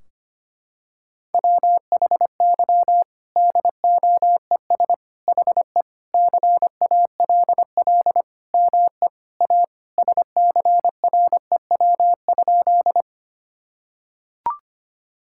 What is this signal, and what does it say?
Morse code, heard as a keyed tone, about 25 wpm, 700 Hz. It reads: WHY DOES HE CALL ME A SCREW?
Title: Sentences from Top 1600 Words 25wpm